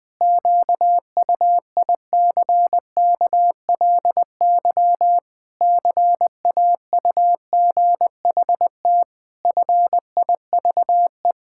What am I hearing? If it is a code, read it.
Morse code, heard as a keyed tone, about 21 wpm, 700 Hz. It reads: QUICKLY CAUGHT FIVE
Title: Pangram Challenge v2 - 20wpm